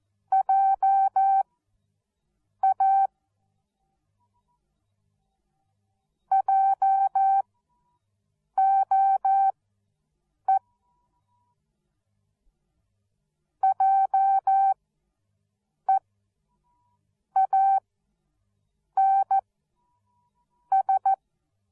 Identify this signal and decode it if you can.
Morse code, heard as a keyed tone, about 14 wpm, spaced as for 6 wpm, 780 Hz. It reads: JA JOE JEANS